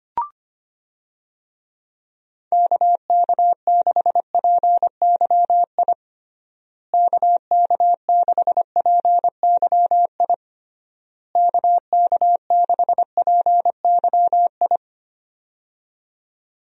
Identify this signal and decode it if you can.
Morse code, heard as a keyed tone, about 25 wpm, 700 Hz. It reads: KK6PYI KK6PYI KK6PYI
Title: USA Call Signs - 6 Characters - Repeated 3x - 25wpm